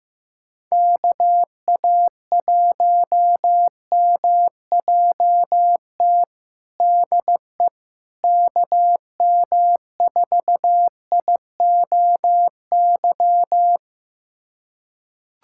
Morse code, heard as a keyed tone, about 16 wpm, 700 Hz. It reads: KA1MJT DE KM4IOY
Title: USA Callsign DE Callsign - 6 Characters - 15wpm